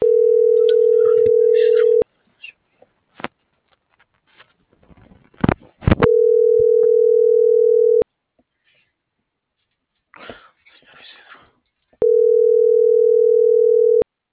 Ringback tone (North America), 440 + 480 Hz.